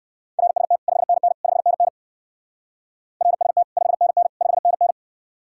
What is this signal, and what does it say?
Morse code, heard as a keyed tone, about 58 wpm, 700 Hz. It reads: RST 5NN 5NN RST 5NN 5NN